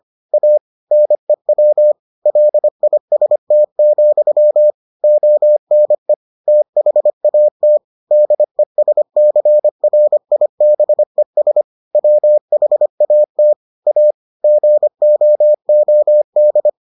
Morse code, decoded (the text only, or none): A NEW LIST, ONE THAT DESCRIBES WHAT A GOOD